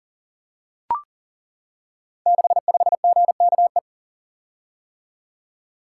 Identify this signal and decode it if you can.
Morse code, heard as a keyed tone, about 40 wpm, 700 Hz. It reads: 65GKE
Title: Code Groups - 5 Letter:Numbers - 40wpm